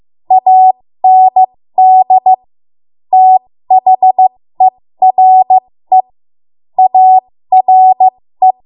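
Morse code, heard as a keyed tone, about 15 wpm, spaced as for 13 wpm, 750 Hz. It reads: AND THERE ARE
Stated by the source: ARRL Morse code practice at 13 WPM.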